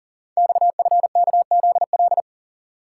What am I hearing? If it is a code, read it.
Morse code, heard as a keyed tone, about 39 wpm, 690 Hz. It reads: XFKZL
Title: Code Groups - 5 Letters - 40wpm